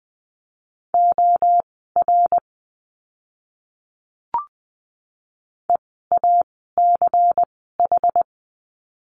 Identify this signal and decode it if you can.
Morse code, heard as a keyed tone, about 20 wpm, spaced as for 15 wpm, 700 Hz. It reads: OR EACH
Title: 192 Lesson 31f - Letter v - Words 20wpm